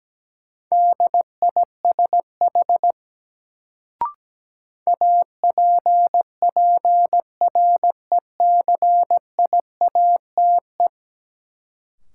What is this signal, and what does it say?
Morse code, heard as a keyed tone, about 17 wpm, 700 Hz. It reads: DISH APPRECIATE
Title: Sentences from Top 1300 Words - New Word Drill Alt - 17wpm